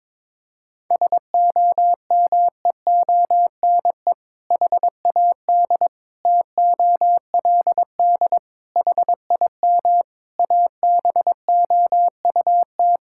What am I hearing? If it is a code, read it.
Morse code, heard as a keyed tone, about 21 wpm, 700 Hz. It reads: SOMEONE HAD TOLD HIM ABOUT
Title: Sentences from Top 1000 Words - Review Part 6 - 22wpm